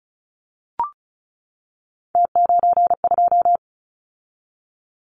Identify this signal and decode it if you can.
Morse code, heard as a keyed tone, about 34 wpm, 700 Hz. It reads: T92
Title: Callsign Prefixes for Top 100 DXCC - 35wpm